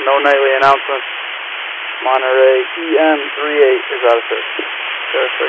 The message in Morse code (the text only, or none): none